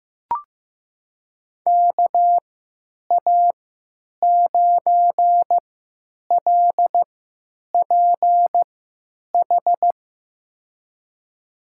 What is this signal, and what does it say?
Morse code, heard as a keyed tone, about 15 wpm, spaced as for 9 wpm, 700 Hz. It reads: KA9LPH
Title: USA Call Signs - 3x Character Spacing - 15wpm